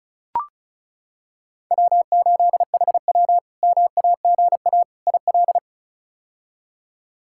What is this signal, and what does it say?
Morse code, heard as a keyed tone, about 36 wpm, 700 Hz. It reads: W8HW MAGA IL
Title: K1USN SST QSO Practice - 35wpm